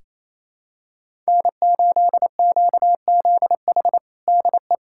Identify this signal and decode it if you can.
Morse code, heard as a keyed tone, about 27 wpm, 700 Hz. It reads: N8QZH DE
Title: USA Callsign DE Callsign - 5 Characters - 28wpm